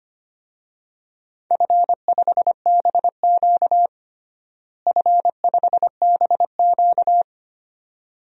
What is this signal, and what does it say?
Morse code, heard as a keyed tone, about 25 wpm, 700 Hz. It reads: F5BQ F5BQ
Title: DXCC-100 Call Signs - 4 Characters - Repeated 2x - 25wpm